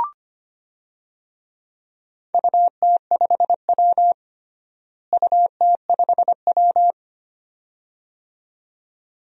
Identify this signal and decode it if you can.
Morse code, heard as a keyed tone, about 25 wpm, 700 Hz. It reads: UT5W UT5W
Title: DXCC-100 Call Signs - 4 Characters - Repeated 2x - 25wpm